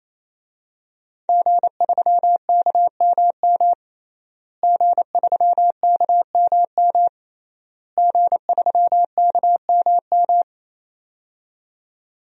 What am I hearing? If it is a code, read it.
Morse code, heard as a keyed tone, about 28 wpm, 700 Hz. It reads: G3KMM G3KMM G3KMM
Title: DXCC-100 Call Signs - 5 Characters - Repeated 3x - 28wpm